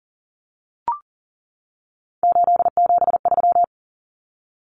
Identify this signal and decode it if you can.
Morse code, heard as a keyed tone, about 40 wpm, 700 Hz. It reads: 873
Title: Numbers - 3-Digit 40wpm